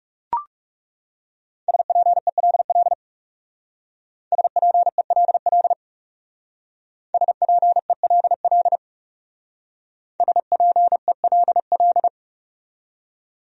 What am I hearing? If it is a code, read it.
Morse code, heard as a keyed tone, about 36 wpm, 700 Hz. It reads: SPELL SPELL SPELL SPELL
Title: Speed-Racing Top 300 Words 30wpm